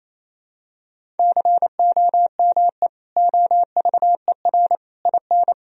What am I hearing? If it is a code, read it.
Morse code, heard as a keyed tone, about 28 wpm, 700 Hz. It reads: COME OVER IN
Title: Sentences from Top 1000 Words - Review Part 1 - 28wpm